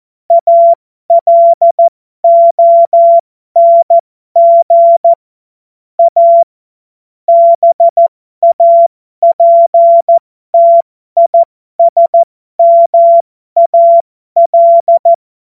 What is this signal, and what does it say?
Morse code, heard as a keyed tone, about 14 wpm, spaced as for 12 wpm, 680 Hz. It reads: ALONG A BAPTISMAL
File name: Morsekurs / Witze